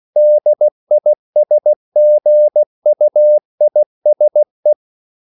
Morse code, heard as a keyed tone, about 16 wpm, 600 Hz. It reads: DISGUISE